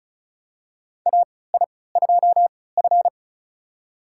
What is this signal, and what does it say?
Morse code, heard as a keyed tone, about 34 wpm, spaced as for 20 wpm, 700 Hz. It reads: AI2F
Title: USA Call Signs - 4 Characters - 3x Character Spacing - 35wpm